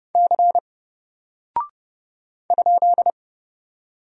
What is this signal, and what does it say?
Morse code, heard as a keyed tone, about 30 wpm, 700 Hz. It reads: C ?
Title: Single Character (Letter:Number:Punctuation:Prosign) 30wpm